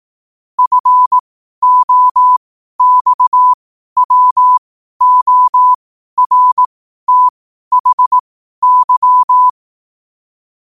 Morse code, heard as a keyed tone, about 18 wpm, spaced as for 12 wpm, 1000 Hz. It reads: FOXWORTHY
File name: qotdfiles / 12wpm